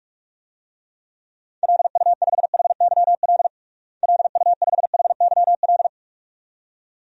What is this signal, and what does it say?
Morse code, heard as a keyed tone, about 46 wpm, 700 Hz. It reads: LU5HYL LU5HYL
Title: DXCC-100 Call Signs - 6 Characters - Repeated 2x - 45wpm